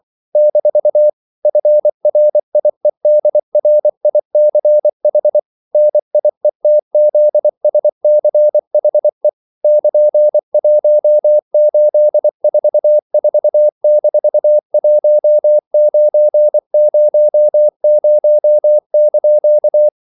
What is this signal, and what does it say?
Morse code, heard as a keyed tone, about 24 wpm, 600 Hz. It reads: - FRIEDRICH NIETZSCHE <KN>1844-1900)
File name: CW / Practice / EnglishQuotations0224